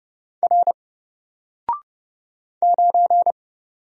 Morse code, heard as a keyed tone, about 30 wpm, 700 Hz. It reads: R 9
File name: Single-Letter-Number-30wpm